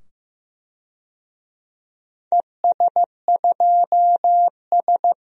Morse code, heard as a keyed tone, about 15 wpm, 700 Hz. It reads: ES2S
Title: DXCC-100 Call Signs - 4 Characters - 15wpm